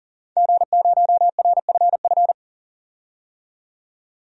Morse code, heard as a keyed tone, about 40 wpm, 700 Hz. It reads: G0RFF
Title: DXCC-100 Call Signs - 5 Characters - 40wpm